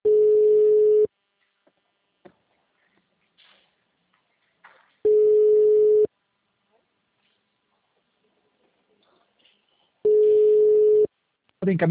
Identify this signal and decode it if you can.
Ringback tone (Europe, 425 Hz), with speech.
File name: out- -2001- - - .0